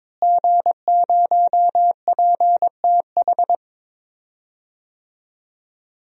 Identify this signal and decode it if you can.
Morse code, heard as a keyed tone, about 22 wpm, 700 Hz. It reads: G0PTH
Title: DXCC-100 Call Signs - 5 Characters - Repeated 2x - 22wpm